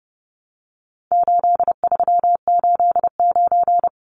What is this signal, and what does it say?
Morse code, heard as a keyed tone, about 30 wpm, 700 Hz. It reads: 8389
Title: Numbers - 4-Digit 30wpm